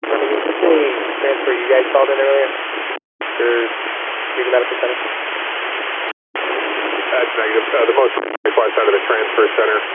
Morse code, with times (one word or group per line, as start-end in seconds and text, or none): none